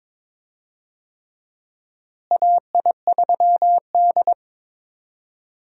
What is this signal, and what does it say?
Morse code, heard as a keyed tone, about 22 wpm, 700 Hz. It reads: AI3D
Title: USA Call Signs - 4 Characters - Repeated 3x - 22wpm